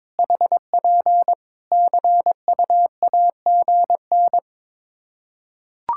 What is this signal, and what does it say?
Morse code, heard as a keyed tone, about 22 wpm, 700 Hz. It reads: HP CUAGN
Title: Practice QSO - Exchange 4 - Ending - 22wpm